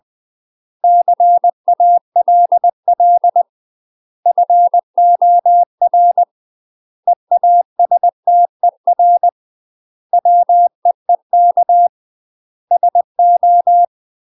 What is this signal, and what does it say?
Morse code, heard as a keyed tone, about 20 wpm, 700 Hz. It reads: CALL FOR EASTER WEEK SO
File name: UK MASTER QSOs TRAINING / 20WPM / UK MASTER QSOs 0036